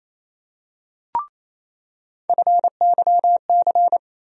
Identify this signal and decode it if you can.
Morse code, heard as a keyed tone, about 28 wpm, 700 Hz. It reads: FYC
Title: USA Call Sign Postfixes - 3 Letters 28wpm